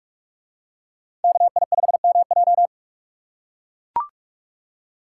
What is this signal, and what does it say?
Morse code, heard as a keyed tone, about 44 wpm, 700 Hz. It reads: KI5MJ